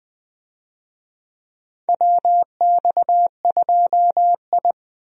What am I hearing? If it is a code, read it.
Morse code, heard as a keyed tone, about 20 wpm, 700 Hz. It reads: WX2I
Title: USA Call Signs - 4 Characters - 20wpm